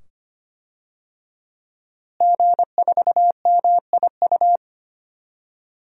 Morse code, heard as a keyed tone, about 25 wpm, 700 Hz. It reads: G4MIU
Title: DXCC-100 Call Signs - 5 Characters - Repeated 2x - 25wpm